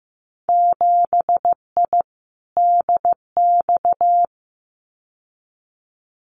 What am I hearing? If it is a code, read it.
Morse code, heard as a keyed tone, about 15 wpm, 700 Hz. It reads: 7I DX
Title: Winter Field Day Exchanges Encore 15wpm